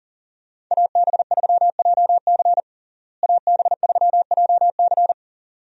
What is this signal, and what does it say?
Morse code, heard as a keyed tone, about 40 wpm, 700 Hz. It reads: AB3JC AB3JC
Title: USA Call Signs - 5 Characters - Repeated 2x - 40wpm